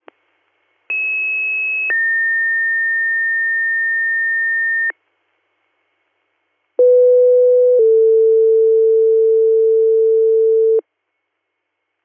Two-tone page, heard as a one-second tone, then a three-second tone, about 2600 then 1800 Hz, about 500 then 450 Hz. It is heard from 0.9 to 10.8 s.